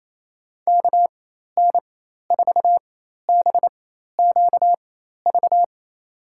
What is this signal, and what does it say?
Morse code, heard as a keyed tone, about 28 wpm, spaced as for 13 wpm, 700 Hz. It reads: KN4BQV
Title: USA Call Signs - 6 Characters - 4x Character Spacing - 28wpm